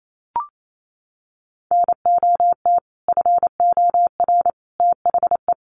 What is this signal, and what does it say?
Morse code, heard as a keyed tone, about 28 wpm, 700 Hz. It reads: NOT FOR THE
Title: Sentences from Top 600 Words 28wpm